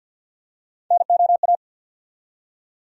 Morse code, heard as a keyed tone, about 50 wpm, 700 Hz. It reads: NOA
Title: USA Call Sign Postfixes - 3 Letters 50wpm